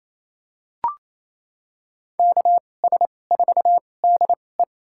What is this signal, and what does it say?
Morse code, heard as a keyed tone, about 28 wpm, spaced as for 20 wpm, 700 Hz. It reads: KS4DE
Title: USA Call Signs - 5 Characters - 2x Character Spacing - 28wpm